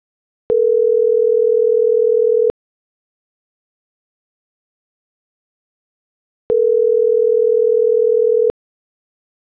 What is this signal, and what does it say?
Ringback tone (North America), 440 + 480 Hz.